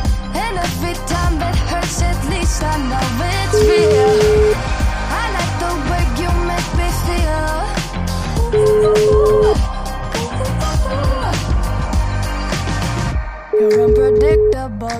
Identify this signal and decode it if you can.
Ringback tone (Europe, 425 Hz), with noise.